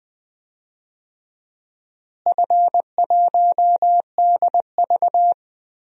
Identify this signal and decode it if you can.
Morse code, heard as a keyed tone, about 21 wpm, 700 Hz. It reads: F1DV
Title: DXCC-100 Call Signs - 4 Characters - Repeated 2x - 20wpm